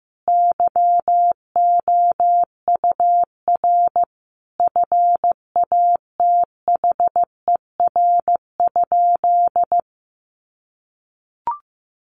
Morse code, heard as a keyed tone, about 15 wpm, 700 Hz. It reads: YOUR FATHER?
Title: Sentences from Top 1300 Words 15wpm